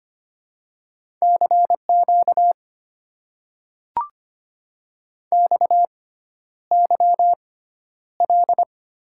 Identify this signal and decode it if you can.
Morse code, heard as a keyed tone, about 25 wpm, spaced as for 11 wpm, 700 Hz. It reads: CQ XYL
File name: Top 90 QSO Elements - 6x Character Spacing - 25wpm